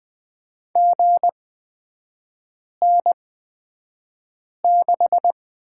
Morse code, heard as a keyed tone, about 20 wpm, spaced as for 5 wpm, 700 Hz. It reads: GN6
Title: 250 Lesson 40e - BK - Three Characters 20wpm